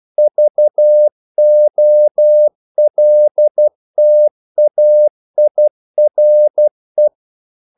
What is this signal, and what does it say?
Morse code, heard as a keyed tone, about 12 wpm, 600 Hz. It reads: VOLTAIRE
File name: CW / Practice / EnglishQuotations0312